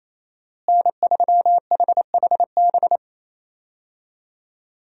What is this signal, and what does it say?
Morse code, heard as a keyed tone, about 27 wpm, 700 Hz. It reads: N3HHB
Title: USA Call Signs - 5 Characters - Repeated 2x - 28wpm